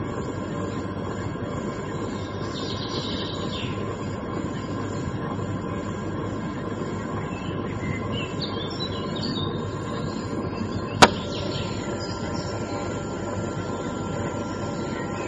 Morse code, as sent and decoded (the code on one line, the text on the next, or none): none